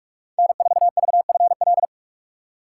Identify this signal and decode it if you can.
Morse code, heard as a keyed tone, about 44 wpm, 690 Hz. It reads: N4VFL
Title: USA Call Signs - 5 Characters - 6x Character Spacing - 45wpm